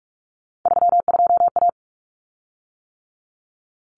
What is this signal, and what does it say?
Morse code, heard as a keyed tone, about 44 wpm, 700 Hz. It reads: 32A
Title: Common Cut Numbers - 3-Digit 45wpm